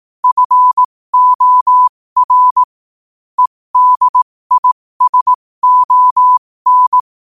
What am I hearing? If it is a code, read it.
Morse code, heard as a keyed tone, about 18 wpm, spaced as for 15 wpm, 1000 Hz. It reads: FOR EDISON
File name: qotdfiles / 15wpm